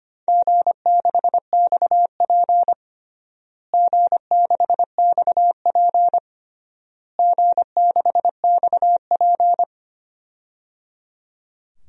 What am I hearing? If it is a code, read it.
Morse code, heard as a keyed tone, about 25 wpm, 700 Hz. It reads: G6XP G6XP G6XP
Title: DXCC-100 Call Signs - 4 Characters - Repeated 3x - 25wpm